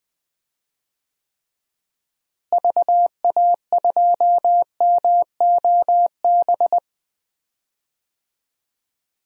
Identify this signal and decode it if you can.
Morse code, heard as a keyed tone, about 20 wpm, 700 Hz. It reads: VA2MOB
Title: DXCC-100 Call Signs - 6 Characters - Repeated 3x - 20wpm